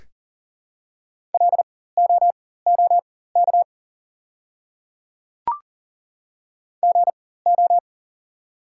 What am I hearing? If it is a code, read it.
Morse code, heard as a keyed tone, about 39 wpm, spaced as for 19 wpm, 700 Hz. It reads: LOOK GO
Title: 4x Character Spacing - Top 200 Words - 40wpm